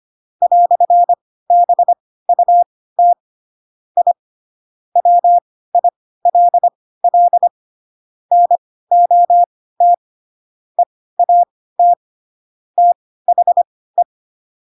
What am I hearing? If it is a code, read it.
Morse code, heard as a keyed tone, about 25 wpm, spaced as for 16 wpm, 700 Hz. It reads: "BUT I WILL NOT EAT THE
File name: Pinocchio / Pinocchio ch7 10 25 ik1hge